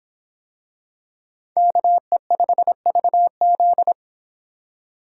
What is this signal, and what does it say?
Morse code, heard as a keyed tone, about 25 wpm, 700 Hz. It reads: KE5VZ